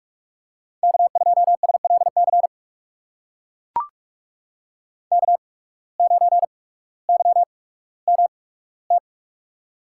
Morse code, heard as a keyed tone, about 44 wpm, spaced as for 19 wpm, 700 Hz. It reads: K2SLC K9YMT
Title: USA Call Signs - 5 Characters - 8x Character Spacing - 45wpm